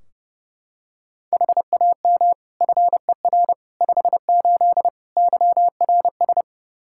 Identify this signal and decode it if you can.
Morse code, heard as a keyed tone, about 30 wpm, 700 Hz. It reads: HAM FER 58 YRS
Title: Practice QSO - Exchange 3 - More Info - 30wpm